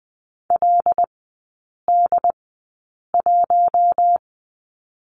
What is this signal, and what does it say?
Morse code, heard as a keyed tone, about 19 wpm, spaced as for 8 wpm, 690 Hz. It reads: LD1
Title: 087 Lesson 15e - Number 5 - Three Characters 20wpm